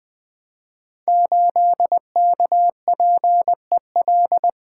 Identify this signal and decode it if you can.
Morse code, heard as a keyed tone, about 20 wpm, 700 Hz. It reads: 8KPEL
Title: Code Groups - 5 Letter:Numbers - 20wpm